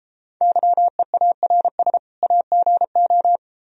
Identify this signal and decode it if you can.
Morse code, heard as a keyed tone, about 31 wpm, 700 Hz. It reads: YEARS AGO